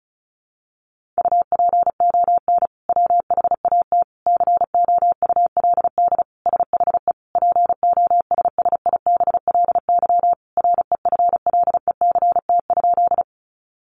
Morse code, heard as a keyed tone, about 34 wpm, 700 Hz. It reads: UPON WHAT COULD SHE POSSIBLY REFLECT?
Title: Sentences from Top 1500 Words 35wpm